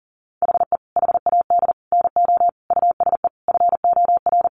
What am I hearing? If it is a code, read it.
Morse code, heard as a keyed tone, about 40 wpm, 700 Hz. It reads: HE HAD NO USE FOR
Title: Sentences from Top 1200 Words 40wpm